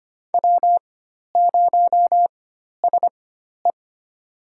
Morse code, heard as a keyed tone, about 25 wpm, spaced as for 12 wpm, 700 Hz. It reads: W0SE
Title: USA Call Signs - 4 Characters - 4x Character Spacing - 25wpm